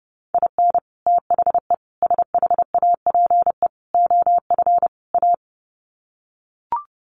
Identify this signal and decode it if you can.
Morse code, heard as a keyed tone, about 30 wpm, 700 Hz. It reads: IN THE SHAPE OF A
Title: Sets of 5 Words Encore 30wpm